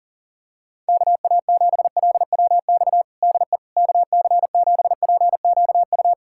Morse code, heard as a keyed tone, about 40 wpm, 700 Hz. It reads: KA7LWX DE KC7PQU
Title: USA Callsign DE Callsign - 6 Characters - 40wpm